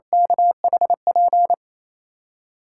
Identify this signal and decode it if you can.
Morse code, heard as a keyed tone, about 27 wpm, 700 Hz. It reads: KHP
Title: USA Callsign DE Callsign - 6 Characters - 28wpm